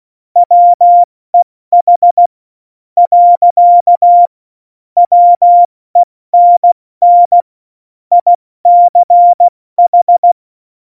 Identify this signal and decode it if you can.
Morse code, heard as a keyed tone, about 16 wpm, 700 Hz. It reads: WEH . WENN ICH
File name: Morsekurs / Witze / Witz 09. Januar 2018, 70 F80